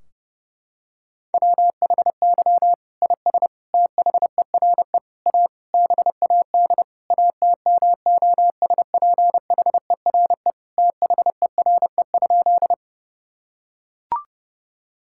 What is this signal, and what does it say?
Morse code, heard as a keyed tone, about 30 wpm, 700 Hz. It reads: WHY IS THERE A BAD ATMOSPHERE THERE?
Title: Sentences from Top 1600 Words 30wpm